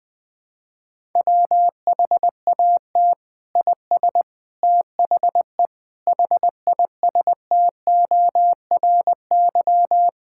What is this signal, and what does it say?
Morse code, heard as a keyed tone, about 20 wpm, 700 Hz. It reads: WHAT IS THE HISTORY
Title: Sentences from Top 1000 Words - Review Part 5 - 20wpm